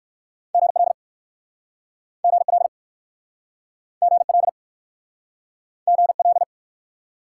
Morse code, heard as a keyed tone, about 58 wpm, 700 Hz. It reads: GL GL GL GL